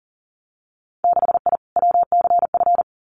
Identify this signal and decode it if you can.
Morse code, heard as a keyed tone, about 38 wpm, 700 Hz. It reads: 6I WCF
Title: Winter Field Day Exchanges Encore 40wpm